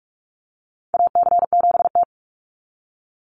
Morse code, heard as a keyed tone, about 44 wpm, 700 Hz. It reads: AC7T
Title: Call Signs Repeated 2x - 45wpm